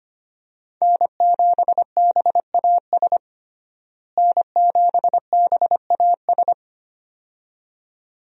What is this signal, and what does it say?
Morse code, heard as a keyed tone, about 25 wpm, 700 Hz. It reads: N7BAS N7BAS
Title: USA Call Signs - 5 Characters - Repeated 2x - 25wpm